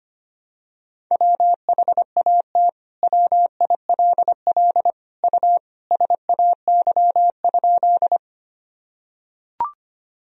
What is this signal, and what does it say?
Morse code, heard as a keyed tone, about 25 wpm, 700 Hz. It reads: WHAT WILL U SAY?
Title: Sentences from Top 200 Words and Abbreviations 25wpm